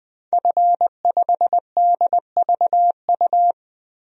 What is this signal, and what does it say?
Morse code, heard as a keyed tone, about 21 wpm, 700 Hz. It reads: F5DVU
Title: DXCC-100 Call Signs - 5 Characters - 20wpm